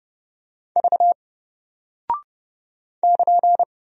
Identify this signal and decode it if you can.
Morse code, heard as a keyed tone, about 30 wpm, 700 Hz. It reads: V <KN>
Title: Single Character (Letter:Number:Punctuation:Prosign) 30wpm